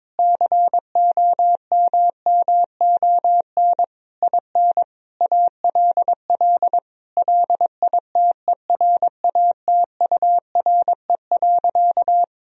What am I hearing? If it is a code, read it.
Morse code, heard as a keyed tone, about 22 wpm, 700 Hz. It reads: COMMON IN ALL LITERATURE.
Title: Sentences from Top 1000 Words - Review Part 3 - 22wpm